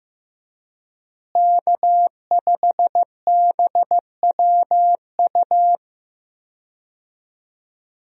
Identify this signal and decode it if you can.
Morse code, heard as a keyed tone, about 15 wpm, 700 Hz. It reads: K5BWU
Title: USA Call Signs - 5 Characters - Repeated 2x - 15wpm